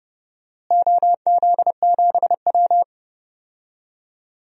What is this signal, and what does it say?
Morse code, heard as a keyed tone, about 30 wpm, 700 Hz. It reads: OZ7W
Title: DXCC-100 Call Signs - 4 Characters - Repeated 2x - 30wpm